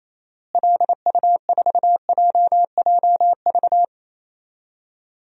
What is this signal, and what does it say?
Morse code, heard as a keyed tone, about 28 wpm, 700 Hz. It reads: LU4JJV
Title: DXCC-100 Call Signs - 6 Characters - Repeated 3x - 28wpm